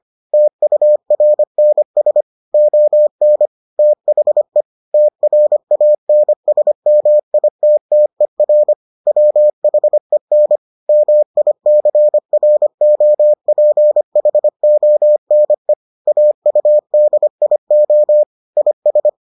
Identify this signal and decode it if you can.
Morse code, heard as a keyed tone, about 25 wpm, 600 Hz. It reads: TURNS ON THE TRANSMITTER WHEN MICROPHONE AUDIO IS